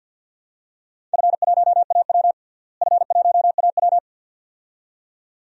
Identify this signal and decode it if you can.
Morse code, heard as a keyed tone, about 49 wpm, 700 Hz. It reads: F1AW F1AW
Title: DXCC-100 Call Signs - 4 Characters - Repeated 2x - 50wpm